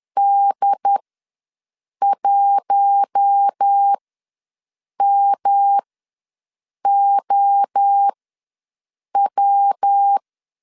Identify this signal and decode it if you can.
Morse code, heard as a keyed tone, about 10 wpm, spaced as for 6 wpm, 800 Hz. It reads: D1MOW
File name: LEZIONI / 6 PKX0 / 6 P ws 10wpm 3751